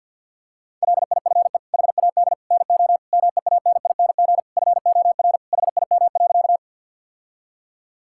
Morse code, heard as a keyed tone, about 51 wpm, 700 Hz. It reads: LIFE HAD NO MEANING FOR HIM.
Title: Sentences from Top 1000 Words - Review Part 6 - 50wpm